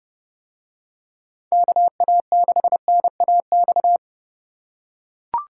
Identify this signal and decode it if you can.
Morse code, heard as a keyed tone, about 29 wpm, 700 Hz. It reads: KA6NAX